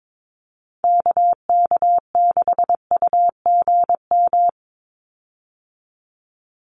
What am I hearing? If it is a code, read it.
Morse code, heard as a keyed tone, about 22 wpm, 700 Hz. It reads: KK6UGM
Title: USA Call Signs 22wpm